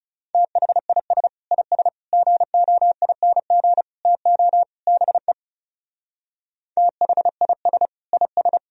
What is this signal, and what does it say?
Morse code, heard as a keyed tone, about 36 wpm, 700 Hz. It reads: THIS IS GOING TO BE THIS IS
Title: Speed-Racing - Sets of 5 Words 30wpm